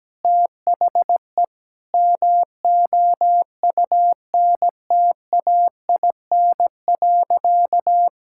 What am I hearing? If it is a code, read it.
Morse code, heard as a keyed tone, about 17 wpm, 700 Hz. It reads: THE MOUNTAIN.
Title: Sentences from Top 1000 Words - Review Part 6 - 17wpm